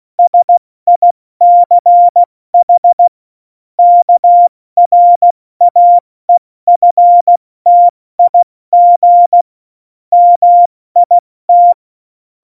Morse code, heard as a keyed tone, about 16 wpm, 700 Hz. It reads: SICH KRAEFTIG MIT
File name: Morsekurs / Witze / Witz 17.Juli 2018, 70 F80